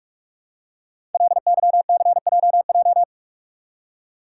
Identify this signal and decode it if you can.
Morse code, heard as a keyed tone, about 44 wpm, 700 Hz. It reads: LYCJJ
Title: Code Groups - 5 Letter:Numbers - 45wpm